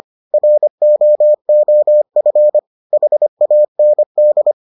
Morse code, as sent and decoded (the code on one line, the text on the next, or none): .-. --- --- ..-. / .... .- -. -..
ROOF HAND